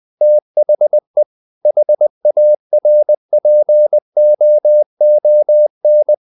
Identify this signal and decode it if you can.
Morse code, heard as a keyed tone, about 20 wpm, 600 Hz. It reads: THE HARPOON
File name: the old man and the sea / the old man and the sea part 1 20wpm